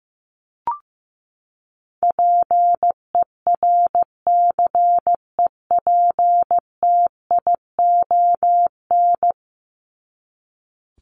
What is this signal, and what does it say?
Morse code, heard as a keyed tone, about 15 wpm, 700 Hz. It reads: PERCEPTION
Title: Top 1751-2000 Words 15wpm